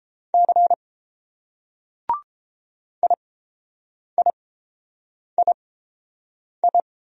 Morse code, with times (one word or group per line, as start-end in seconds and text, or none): 0.3-0.7 C
3.0-6.8 IIII